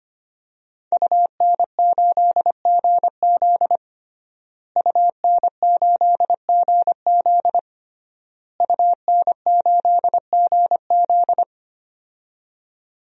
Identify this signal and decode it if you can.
Morse code, heard as a keyed tone, about 25 wpm, 700 Hz. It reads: UN8GZ UN8GZ UN8GZ
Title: DXCC-100 Call Signs - 5 Characters - Repeated 3x - 25wpm